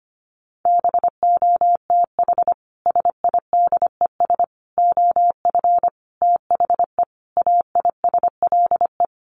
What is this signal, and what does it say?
Morse code, heard as a keyed tone, about 25 wpm, 700 Hz. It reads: BOTH SIDES OF THE AISLE
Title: Sets of 5 Words Encore2 25wpm